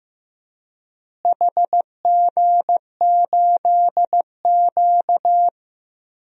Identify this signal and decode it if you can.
Morse code, heard as a keyed tone, about 16 wpm, 700 Hz. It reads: HG8Q
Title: DXCC-100 Call Signs - 4 Characters - Repeated 3x - 15wpm